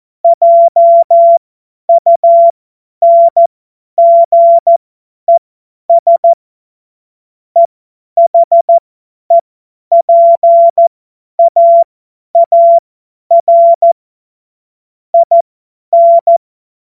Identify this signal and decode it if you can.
Morse code, heard as a keyed tone, about 14 wpm, spaced as for 10 wpm, 680 Hz. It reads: JUNGES EHEPAAR IN
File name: Morsekurs / Witze / Witz 06. September 2022 50 F70